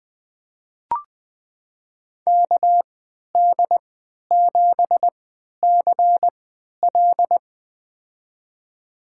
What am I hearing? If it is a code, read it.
Morse code, heard as a keyed tone, about 21 wpm, spaced as for 11 wpm, 700 Hz. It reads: KD7CL
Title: USA Call Signs - 5 Characters - 3x Character Spacing - 20wpm